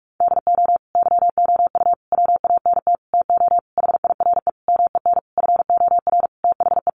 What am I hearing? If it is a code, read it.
Morse code, heard as a keyed tone, about 44 wpm, 700 Hz. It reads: DO YOU WANT TO HIRE MEN FOR THE
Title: Sentences from Top 1400 Words 45wpm